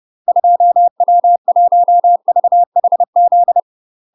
Morse code, heard as a keyed tone, about 30 wpm, 700 Hz. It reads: 2W1VHZ
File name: UK MASTER QSOs TRAINING / 30WPM / UK MASTER QSOs 0044